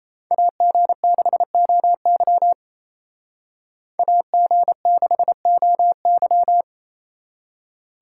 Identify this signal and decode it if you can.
Morse code, heard as a keyed tone, about 30 wpm, 700 Hz. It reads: AG6OY AG6OY
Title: Speed Racing - USA 2x2 Call Signs 28wpm